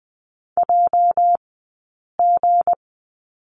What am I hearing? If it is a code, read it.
Morse code, heard as a keyed tone, about 20 wpm, spaced as for 8 wpm, 700 Hz. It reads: JG
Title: 204 Lesson 33d - Letter j - Two Characters 20wpm